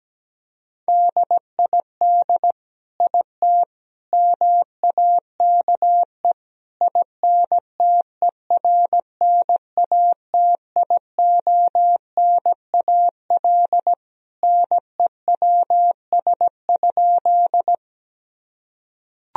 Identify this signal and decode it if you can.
Morse code, heard as a keyed tone, about 17 wpm, 700 Hz. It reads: DID IT MAKE INTERNATIONAL NEWS?
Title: Sentences from Top 1000 Words - Review Part 4 - 17wpm